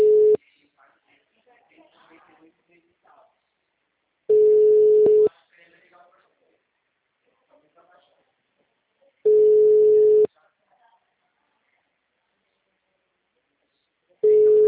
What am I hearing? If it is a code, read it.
Ringback tone (Europe, 425 Hz), with speech.